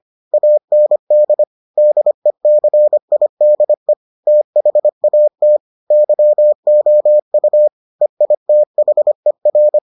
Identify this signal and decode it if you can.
Morse code, heard as a keyed tone, about 25 wpm, 600 Hz. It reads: AND DECIDE THAT YOU EITHER